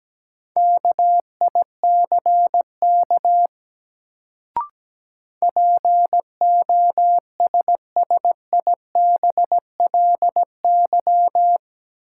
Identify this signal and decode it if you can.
Morse code, heard as a keyed tone, about 17 wpm, 700 Hz. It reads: KICK POSSIBLY
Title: Sentences from Top 1300 Words - New Word Drill Alt - 17wpm